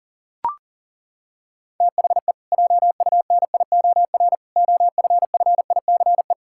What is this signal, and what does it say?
Morse code, heard as a keyed tone, about 40 wpm, 700 Hz. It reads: THE JUNIOR OFFICE
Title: Pangram Challenge v2 - 40wpm